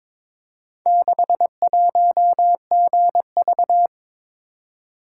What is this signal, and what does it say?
Morse code, heard as a keyed tone, about 22 wpm, 700 Hz. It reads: 61GV
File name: practice / 22wpm / Four Letter-Number Combinations - 22wpm